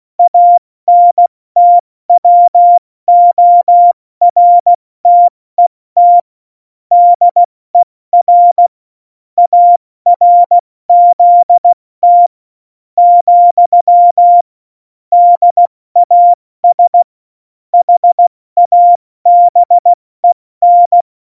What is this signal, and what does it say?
Morse code, heard as a keyed tone, about 16 wpm, 700 Hz. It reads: ANTWORTET DER ARZT , DAS HABEN
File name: Morsekurs / Witze / Witz 13 March 2018, 70 F80